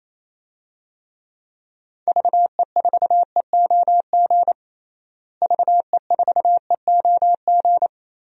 Morse code, heard as a keyed tone, about 28 wpm, 700 Hz. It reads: VE4EOG VE4EOG
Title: DXCC-100 Call Signs - 6 Characters - Repeated 3x - 28wpm